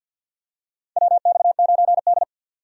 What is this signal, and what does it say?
Morse code, heard as a keyed tone, about 52 wpm, 700 Hz. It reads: WX9D
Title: USA Call Signs - 4 Characters - 50wpm